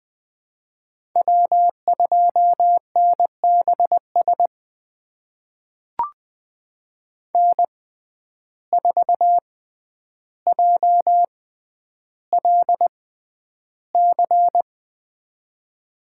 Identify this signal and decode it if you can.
Morse code, heard as a keyed tone, about 20 wpm, spaced as for 10 wpm, 700 Hz. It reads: W2NBS N4JLC
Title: USA Call Signs - 6x Character Spacing - 20wpm